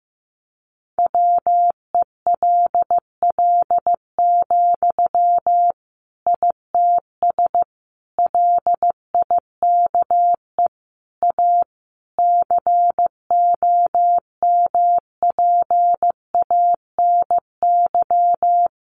Morse code, heard as a keyed tone, about 15 wpm, 700 Hz. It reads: WELL, ITS LIKE A COMPANY
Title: Sentences from Top 600 Words 15wpm